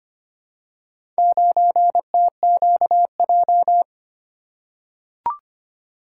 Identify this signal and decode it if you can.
Morse code, heard as a keyed tone, about 25 wpm, 700 Hz. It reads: 9TQJ
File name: practice / 25wpm / Four Letter-Number Combinations - 25wpm